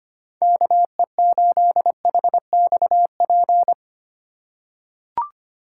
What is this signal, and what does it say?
Morse code, heard as a keyed tone, about 25 wpm, 700 Hz. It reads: KE8HXP
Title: USA Call Signs - 2x Character Spacing - 25wpm